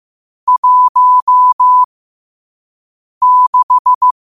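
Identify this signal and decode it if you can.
Morse code, heard as a keyed tone, about 15 wpm, spaced as for 5 wpm, 1000 Hz. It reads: 16
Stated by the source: Morse code quote of the day sent at 5 wpm.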